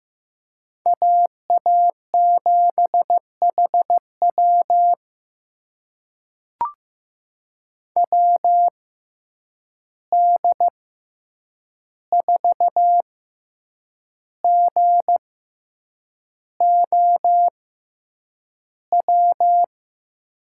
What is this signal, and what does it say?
Morse code, heard as a keyed tone, about 16 wpm, spaced as for 7 wpm, 700 Hz. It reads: AA7HW WD4GOW
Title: USA Call Signs - 6x Character Spacing - 15wpm